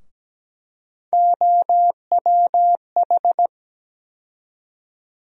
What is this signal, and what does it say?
Morse code, heard as a keyed tone, about 17 wpm, 700 Hz. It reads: OWH
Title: USA Call Sign Postfixes - 3 Letters 17wpm